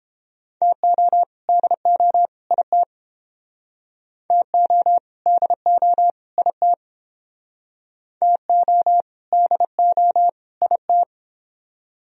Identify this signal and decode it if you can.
Morse code, heard as a keyed tone, about 31 wpm, 700 Hz. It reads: TO DO IT TO DO IT TO DO IT